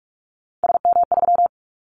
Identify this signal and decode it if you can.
Morse code, heard as a keyed tone, about 46 wpm, 700 Hz. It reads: SM3
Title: Callsign Prefixes for Top 25 DXCC - 45wpm